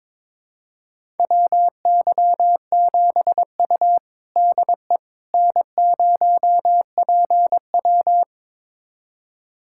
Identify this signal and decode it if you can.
Morse code, heard as a keyed tone, about 21 wpm, 700 Hz. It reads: WY7U DE N0PW
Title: USA Callsign DE Callsign - 4 Characters - 22wpm